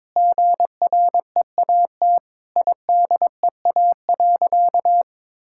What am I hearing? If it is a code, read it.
Morse code, heard as a keyed tone, about 21 wpm, 700 Hz. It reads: GREAT IDEA.
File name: Sentences with 7-Word Phrases - 22wpm